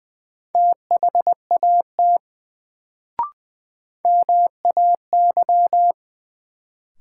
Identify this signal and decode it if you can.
Morse code, heard as a keyed tone, about 20 wpm, 700 Hz. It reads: THAT MAY
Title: Sets of 2 Words - New Word Drill - 20wpm